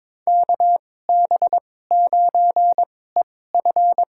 Morse code, heard as a keyed tone, about 22 wpm, spaced as for 16 wpm, 700 Hz. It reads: KB9EF
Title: USA Call Signs - 5 Characters - 2x Character Spacing - 22wpm